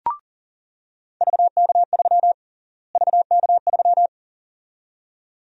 Morse code, heard as a keyed tone, about 40 wpm, 700 Hz. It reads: VK3 VK3
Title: USA Call Sign Prefixes - Repeated 2x - 40wpm